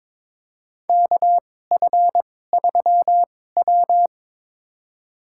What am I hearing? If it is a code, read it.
Morse code, heard as a keyed tone, about 22 wpm, spaced as for 16 wpm, 700 Hz. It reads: KF3W
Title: USA Call Signs - 4 Characters - 2x Character Spacing - 22wpm